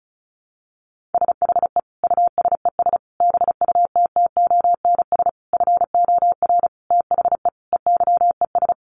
Morse code, heard as a keyed tone, about 34 wpm, 700 Hz. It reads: SHE USES BUTTONS FOR THE EYES
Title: Easy Phrases 35wpm